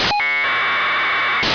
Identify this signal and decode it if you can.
Packet radio (AX.25, as in APRS).
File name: phonesat25 0914utc